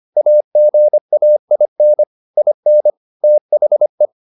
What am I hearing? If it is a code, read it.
Morse code, heard as a keyed tone, about 25 wpm, 600 Hz. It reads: AGAIN IN THE